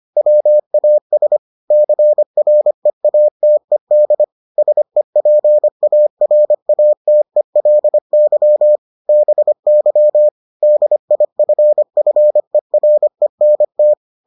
Morse code, heard as a keyed tone, about 25 wpm, 600 Hz. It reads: WAS CREATED SEPARATELY BY DIFFERENT